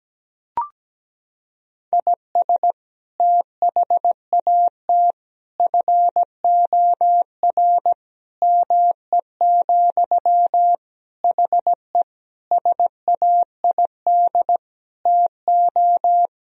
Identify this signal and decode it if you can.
Morse code, heard as a keyed tone, about 17 wpm, 700 Hz. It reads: IS THAT FOR ME, HE SAID TO
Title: Sentences from Top 1000 Words - Review Part 1 - 17wpm